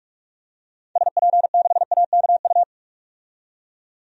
Morse code, heard as a keyed tone, about 46 wpm, 700 Hz. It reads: SP6AKU
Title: DXCC-100 Call Signs - 6 Characters - 45wpm